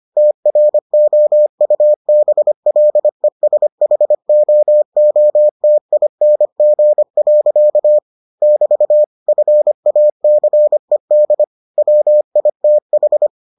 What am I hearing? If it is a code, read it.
Morse code, heard as a keyed tone, about 25 wpm, 600 Hz. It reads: TROUBLESHOOTING. <BT> FACED WITH